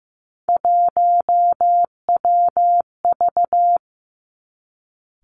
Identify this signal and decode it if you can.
Morse code, heard as a keyed tone, about 15 wpm, 700 Hz. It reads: 1WV
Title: Three Challenging Letter-Number Combinations 15wpm